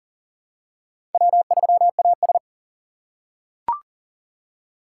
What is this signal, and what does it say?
Morse code, heard as a keyed tone, about 38 wpm, 700 Hz. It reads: W3AS
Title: USA Call Signs - 4 Characters - 4x Character Spacing - 40wpm